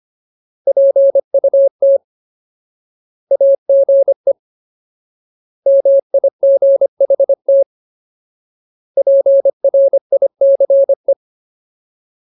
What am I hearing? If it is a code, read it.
Morse code, heard as a keyed tone, about 25 wpm, 550 Hz. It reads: PUT AGE MIGHT PRICE
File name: Ultimate Word List / 25WPM Ultimate